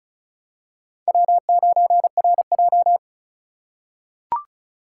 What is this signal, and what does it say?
Morse code, heard as a keyed tone, about 36 wpm, 700 Hz. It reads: W9RJ
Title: USA Call Signs - 4 Characters - 4x Character Spacing - 35wpm